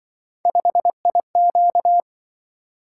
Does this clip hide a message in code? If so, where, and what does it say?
Yes: Morse code from 0.5 to 2.0 s, heard as a keyed tone, about 26 wpm, 700 Hz. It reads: 5IQ